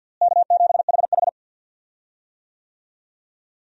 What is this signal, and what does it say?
Morse code, heard as a keyed tone, about 53 wpm, 690 Hz. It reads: K7HH
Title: USA Call Signs - 4 Characters - Repeated 2x - 50wpm